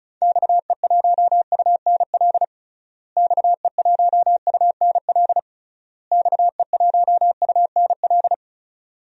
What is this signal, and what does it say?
Morse code, heard as a keyed tone, about 34 wpm, 700 Hz. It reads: XE1UNL XE1UNL XE1UNL
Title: DXCC-100 Call Signs - 6 Characters - Repeated 3x - 35wpm